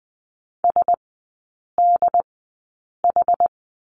Morse code, heard as a keyed tone, about 20 wpm, spaced as for 8 wpm, 700 Hz. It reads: SDH
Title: 074 Lesson 13e - Letter L - Three Characters 20wpm